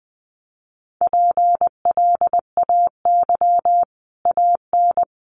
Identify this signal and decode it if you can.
Morse code, heard as a keyed tone, about 20 wpm, 700 Hz. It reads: PLAY AN
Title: Sets of 5 Words Encore 20wpm